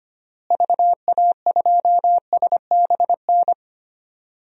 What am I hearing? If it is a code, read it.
Morse code, heard as a keyed tone, about 25 wpm, 700 Hz. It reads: VA2SBN
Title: DXCC-100 Call Signs - 6 Characters - Repeated 2x - 25wpm